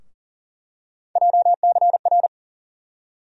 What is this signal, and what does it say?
Morse code, heard as a keyed tone, about 40 wpm, 700 Hz. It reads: JCR
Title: USA Call Sign Postfixes - 3 Letters 40wpm